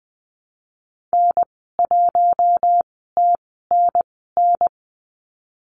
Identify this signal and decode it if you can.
Morse code, heard as a keyed tone, about 20 wpm, spaced as for 14 wpm, 700 Hz. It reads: N1TNN
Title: 048 Lesson 9g - Number 4 - Call Signs 20wpm